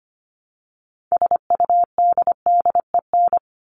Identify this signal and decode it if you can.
Morse code, heard as a keyed tone, about 25 wpm, 700 Hz. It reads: SUDDEN
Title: LY Suffix - Top 100 Words with Base Words 25wpm